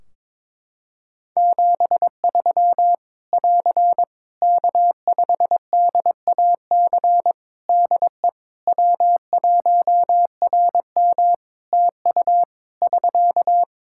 Morse code, heard as a keyed tone, about 22 wpm, 700 Hz. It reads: 73 <AR> K5DAC DE W1RM TU <SK>
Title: Practice QSO - Exchange 4 - Ending - 22wpm